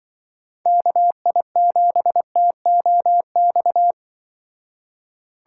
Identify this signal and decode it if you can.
Morse code, heard as a keyed tone, about 24 wpm, 700 Hz. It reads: KI7TOX
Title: Speed Racing - USA 2x3 Call Signs 20wpm